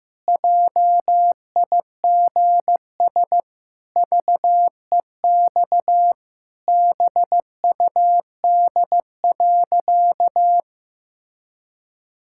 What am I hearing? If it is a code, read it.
Morse code, heard as a keyed tone, about 15 wpm, 700 Hz. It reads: JIGS VEX BUD.
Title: Pangram Challenge v2 - 15wpm